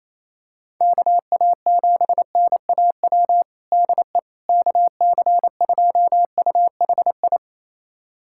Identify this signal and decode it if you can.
Morse code, heard as a keyed tone, about 28 wpm, 700 Hz. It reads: KA7NAW DE KC2UHI
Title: USA Callsign DE Callsign 28wpm